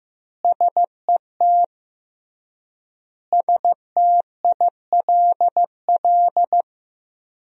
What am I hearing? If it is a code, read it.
Morse code, heard as a keyed tone, about 15 wpm, 700 Hz. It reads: SET STILL
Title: 3x Word-Spacing - Sentences from Top 300 Words - 15wpm